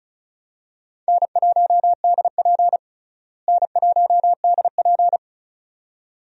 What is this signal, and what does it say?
Morse code, heard as a keyed tone, about 36 wpm, 700 Hz. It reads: N1DP N1DP
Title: USA Call Signs - 4 Characters - Repeated 2x - 35wpm